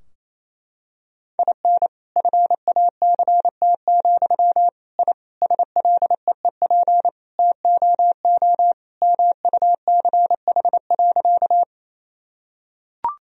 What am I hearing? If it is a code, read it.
Morse code, heard as a keyed tone, about 28 wpm, 700 Hz. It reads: IN FACT, I SLEEP TOO MUCH.
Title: Sentences from Top 1000 Words - Review Part 3 - 28wpm